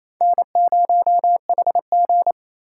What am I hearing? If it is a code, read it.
Morse code, heard as a keyed tone, about 28 wpm, 700 Hz. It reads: N0HG
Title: USA Call Signs - 4 Characters - 3x Character Spacing - 28wpm